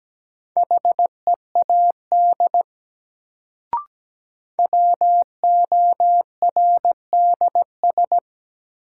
Morse code, heard as a keyed tone, about 17 wpm, 700 Hz. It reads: HEAD WORDS
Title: Sets of 3 Words - New Word Drill - 17wpm